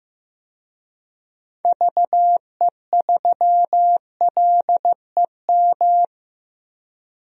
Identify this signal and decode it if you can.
Morse code, heard as a keyed tone, about 15 wpm, 700 Hz. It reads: VE3LEM
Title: DXCC-100 Call Signs - 6 Characters - 15wpm